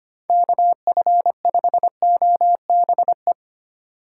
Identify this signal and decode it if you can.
Morse code, heard as a keyed tone, about 25 wpm, 700 Hz. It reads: KF5OBE
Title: USA Call Signs - 6 Characters - 2x Character Spacing - 25wpm